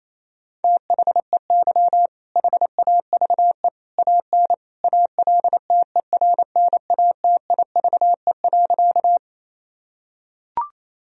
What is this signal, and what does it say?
Morse code, heard as a keyed tone, about 28 wpm, 700 Hz. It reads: THEY HAVE AN ALTERNATIVE.
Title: Sentences from Top 1000 Words - Review Part 4 - 28wpm